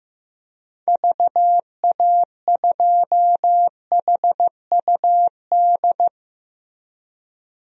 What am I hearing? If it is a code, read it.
Morse code, heard as a keyed tone, about 15 wpm, 700 Hz. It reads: VA2HUD
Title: DXCC-100 Call Signs - 6 Characters - Repeated 2x - 15wpm